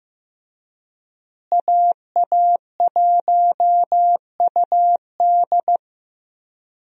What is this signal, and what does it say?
Morse code, heard as a keyed tone, about 15 wpm, 700 Hz. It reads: AA1UD
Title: USA Call Signs - 5 Characters - Repeated 2x - 15wpm